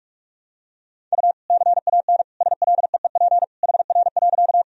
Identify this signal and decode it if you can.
Morse code, heard as a keyed tone, about 46 wpm, 700 Hz. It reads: U CAN SLEEP HR.
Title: Sentences from Top 500 Words and Abbreviations 45wpm